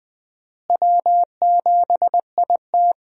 Morse code, heard as a keyed tone, about 21 wpm, 700 Hz. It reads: W7IT
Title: USA Call Signs - 4 Characters - Repeated 3x - 20wpm